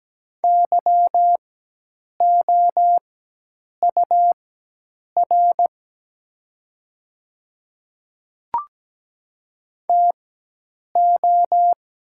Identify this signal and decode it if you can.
Morse code, heard as a keyed tone, about 17 wpm, spaced as for 8 wpm, 700 Hz. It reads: YOUR TO
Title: 4x Character Spacing - Top 100 Words - 17wpm